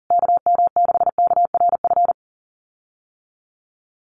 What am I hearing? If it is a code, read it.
Morse code, heard as a keyed tone, about 40 wpm, 700 Hz. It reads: KM6KRF
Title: USA Call Signs Repeated 3x 40wpm